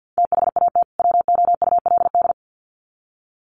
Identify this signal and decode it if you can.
Morse code, heard as a keyed tone, about 50 wpm, 700 Hz. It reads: THAT WOULD
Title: Sets of 2 Words 50wpm